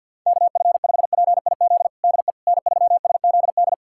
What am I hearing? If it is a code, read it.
Morse code, heard as a keyed tone, about 50 wpm, 700 Hz. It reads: KF5PIZ DE N3SZD
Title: USA Callsign DE Callsign 50wpm